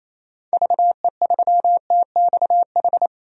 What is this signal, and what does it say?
Morse code, heard as a keyed tone, about 28 wpm, 700 Hz. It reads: VE3TXH
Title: DXCC-100 Call Signs - 6 Characters - 28wpm